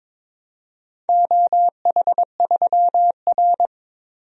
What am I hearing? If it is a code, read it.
Morse code, heard as a keyed tone, about 22 wpm, 700 Hz. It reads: OH3R
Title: DXCC-100 Call Signs - 4 Characters - 22wpm